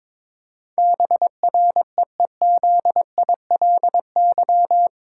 Morse code, heard as a keyed tone, about 22 wpm, 700 Hz. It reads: BREEZILY
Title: Pangram Challenge v2 - 22wpm